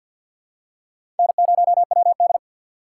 Morse code, heard as a keyed tone, about 47 wpm, 700 Hz. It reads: N0WD
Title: USA Call Signs - 4 Characters - 6x Character Spacing - 50wpm